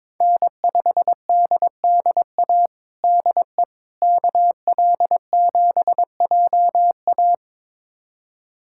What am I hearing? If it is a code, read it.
Morse code, heard as a keyed tone, about 22 wpm, 700 Hz. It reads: N5DDA DE KL7JA
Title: USA Callsign DE Callsign - 5 Characters - 22wpm